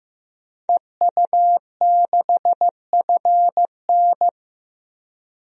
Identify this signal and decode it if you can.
Morse code, heard as a keyed tone, about 15 wpm, 700 Hz. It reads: EU6FN
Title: DXCC-100 Call Signs - 5 Characters - 15wpm